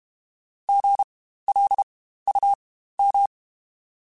Morse code, heard as a keyed tone, about 32 wpm, spaced as for 15 wpm, 800 Hz. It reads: GLUM
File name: Spaced Alphas / Spaced Alphas mp3s / Gs sp @30